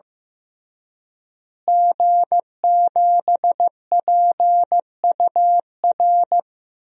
Morse code, heard as a keyed tone, about 15 wpm, 700 Hz. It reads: G7PUR
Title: DXCC-100 Call Signs - 5 Characters - Repeated 3x - 15wpm